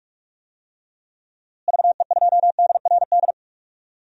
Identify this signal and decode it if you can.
Morse code, heard as a keyed tone, about 43 wpm, 700 Hz. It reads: VE2DRD
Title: DXCC-100 Call Signs - 6 Characters - Repeated 3x - 45wpm